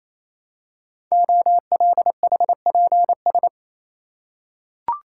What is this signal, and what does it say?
Morse code, heard as a keyed tone, about 28 wpm, 700 Hz. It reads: OLHPS
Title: Code Groups - 5 Letters - 28wpm